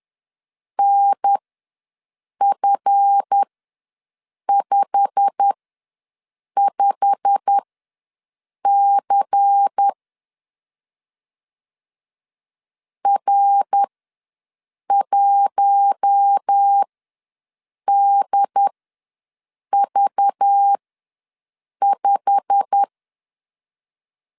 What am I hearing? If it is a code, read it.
Morse code, heard as a keyed tone, about 10 wpm, spaced as for 6 wpm, 800 Hz. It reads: NF55C R1DV5
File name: LEZIONI / 8 YJ45 / 8 YJ45 ws 10wpm 3843